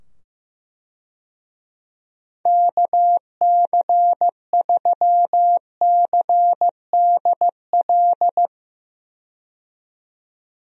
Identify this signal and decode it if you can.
Morse code, heard as a keyed tone, about 15 wpm, 700 Hz. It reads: KC3CDL
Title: USA Call Signs - 6 Characters - 15wpm